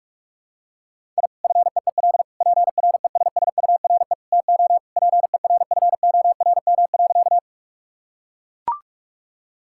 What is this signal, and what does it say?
Morse code, heard as a keyed tone, about 41 wpm, 700 Hz. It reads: I FEEL PRESSURE TO PERFORM.
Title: Sentences from Top 1000 Words - Review Part 6 - 45wpm